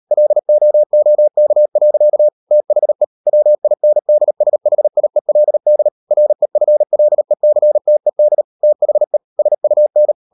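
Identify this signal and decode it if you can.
Morse code, heard as a keyed tone, about 38 wpm, 600 Hz. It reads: LOOK. THE WINDSHIELD REFLECTED THE SUN
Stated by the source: Morse code practice at 35–39 wpm sending English text of Ken Follett's novel Never, from CW med Gnister.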